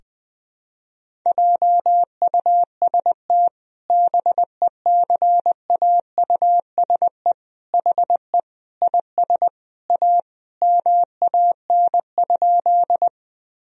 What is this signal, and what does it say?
Morse code, heard as a keyed tone, about 20 wpm, 700 Hz. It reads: JUST BECAUSE HE IS A MAN?
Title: Sentences from Top 1000 Words - Review Part 3 - 20wpm